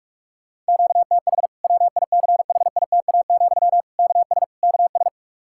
Morse code, heard as a keyed tone, about 41 wpm, 700 Hz. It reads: QTH WICHITA, KS KS
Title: K1USN SST QSO Practice 45wpm